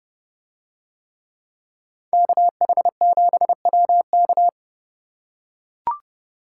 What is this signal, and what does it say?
Morse code, heard as a keyed tone, about 30 wpm, 700 Hz. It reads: KH7WK